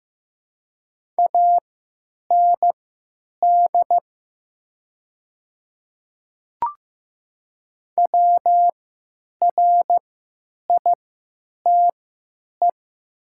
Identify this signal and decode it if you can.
Morse code, heard as a keyed tone, about 15 wpm, spaced as for 9 wpm, 700 Hz. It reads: AND WRITE
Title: 3x Character Spacing - Top 100 Words - 15wpm